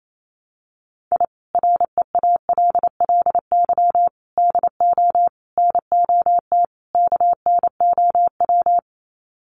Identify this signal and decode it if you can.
Morse code, heard as a keyed tone, about 28 wpm, 700 Hz. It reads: I REALLY DO NOT KNOW
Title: Sets of 5 Words Encore 28wpm